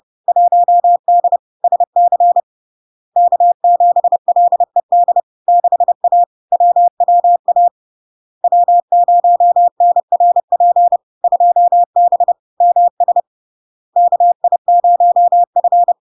Morse code, heard as a keyed tone, about 30 wpm, 700 Hz. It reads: 1D SC K7LED 6A WWA W0NRP 2B MS KI0F
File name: LARK-FD-2022-Log-in-CW-30wpm-700-Hz